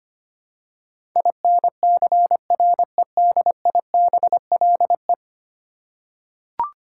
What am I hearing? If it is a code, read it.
Morse code, heard as a keyed tone, about 25 wpm, 700 Hz. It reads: INCREDIBLE
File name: Ten Letter Words - 25wpm